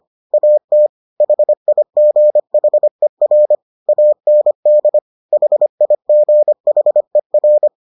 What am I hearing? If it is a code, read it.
Morse code, heard as a keyed tone, about 25 wpm, 600 Hz. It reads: AT HIGHER AND HIGHER